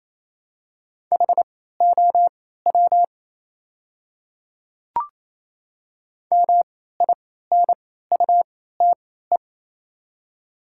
Morse code, heard as a keyed tone, about 28 wpm, spaced as for 16 wpm, 700 Hz. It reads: HOW MINUTE
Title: 3x Character Spacing - Top 500 Words - 28wpm